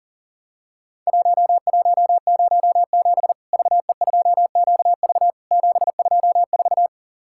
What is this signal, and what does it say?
Morse code, heard as a keyed tone, about 43 wpm, 700 Hz. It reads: 1107 VE2QV 724
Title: Worked All Europe DX Contest - QTC Exchange - 40wpm